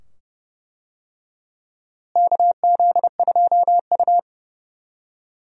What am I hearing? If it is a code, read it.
Morse code, heard as a keyed tone, about 31 wpm, 700 Hz. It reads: KZ2U
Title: USA Call Signs - 4 Characters - Repeated 2x - 30wpm